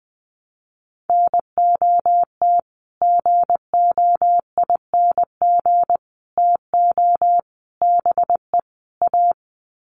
Morse code, heard as a keyed tone, about 20 wpm, 700 Hz. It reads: NOT GOING TO BE A
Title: Sets of 5 Words Encore 20wpm